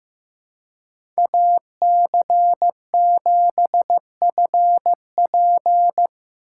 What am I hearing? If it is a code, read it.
Morse code, heard as a keyed tone, about 15 wpm, 700 Hz. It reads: AC7FP
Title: USA Call Signs - 5 Characters - 4x Character Spacing - 15wpm